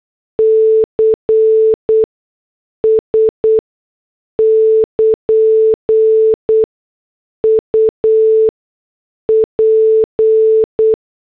Morse code, heard as a keyed tone, about 8 wpm, spaced as for 6 wpm, 440 Hz. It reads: CS <KN> UP